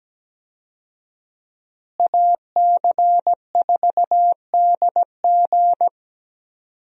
Morse code, heard as a keyed tone, about 17 wpm, 700 Hz. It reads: AC4DG
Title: USA Call Signs - 5 Characters - Repeated 3x - 17wpm